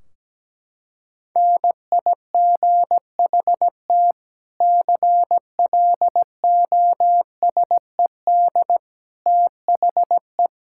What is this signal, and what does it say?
Morse code, heard as a keyed tone, about 17 wpm, 700 Hz. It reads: NIGHT CLOSED THE
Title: Sentences from Top 1600 Words 17wpm